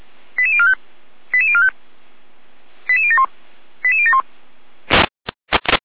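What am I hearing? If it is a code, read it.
Five-tone selective call (ZVEI), heard as quick runs of five tones, tones of about 70 ms. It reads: 88045 88045 88071 88071
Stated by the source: ZVEI five-tone selective call received by SDR, from feuerwehr-orlamuende.org fire brigade server.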